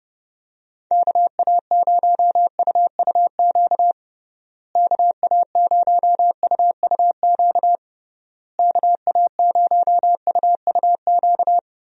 Morse code, heard as a keyed tone, about 32 wpm, 700 Hz. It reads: KA0UUQ KA0UUQ KA0UUQ
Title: USA Call Signs - 6 Characters - Repeated 3x - 30wpm